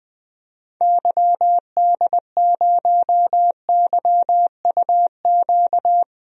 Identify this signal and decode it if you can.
Morse code, heard as a keyed tone, about 20 wpm, 700 Hz. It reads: YD0YUQ
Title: DXCC-100 Call Signs - 6 Characters - Repeated 2x - 20wpm